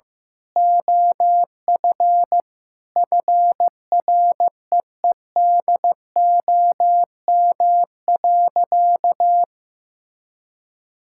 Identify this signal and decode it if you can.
Morse code, heard as a keyed tone, about 15 wpm, 700 Hz. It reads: OF FREEDOM.
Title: Sentences from Top 1600 Words 15wpm